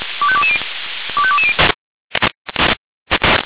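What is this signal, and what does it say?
Five-tone selective call (ZVEI), heard as quick runs of five tones, tones of about 70 ms. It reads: 25220 25220
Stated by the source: ZVEI five-tone selective call received on an SDR.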